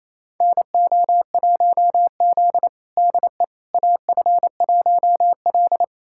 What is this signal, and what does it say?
Morse code, heard as a keyed tone, about 28 wpm, 700 Hz. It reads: NO1Z DE AF1L
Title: USA Callsign DE Callsign - 4 Characters - 28wpm